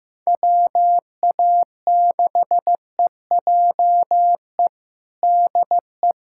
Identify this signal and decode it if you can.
Morse code, heard as a keyed tone, about 15 wpm, 700 Hz. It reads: WA6EJE DE
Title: USA Callsign DE Callsign - 6 Characters - 15wpm